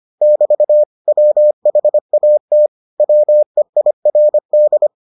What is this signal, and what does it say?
Morse code, heard as a keyed tone, about 25 wpm, 600 Hz. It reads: <BT> WHAT WEIRD